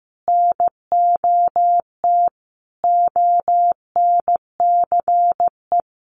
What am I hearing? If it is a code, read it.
Morse code, heard as a keyed tone, about 15 wpm, 700 Hz. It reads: NOT ONCE
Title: Sentences from Top 1500 Words 15wpm